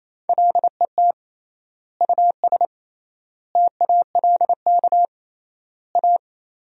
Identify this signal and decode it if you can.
Morse code, heard as a keyed tone, about 28 wpm, 700 Hz. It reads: LET US TALK A
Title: 3x Word-Spacing - Sentences from Top 400 Words - 28wpm